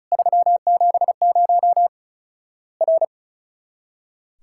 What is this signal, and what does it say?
Morse code, heard as a keyed tone, about 36 wpm, 700 Hz. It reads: 370 R
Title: Worked All Europe DX Contest - QTC Exchange - 35wpm